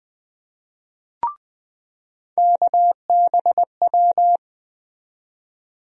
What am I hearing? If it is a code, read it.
Morse code, heard as a keyed tone, about 20 wpm, 700 Hz. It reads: KBW
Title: USA Call Sign Postfixes - 3 Letters 20wpm